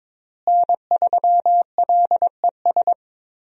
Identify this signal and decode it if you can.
Morse code, heard as a keyed tone, about 21 wpm, 700 Hz. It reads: N3LES
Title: USA Call Signs - 5 Characters - Repeated 2x - 22wpm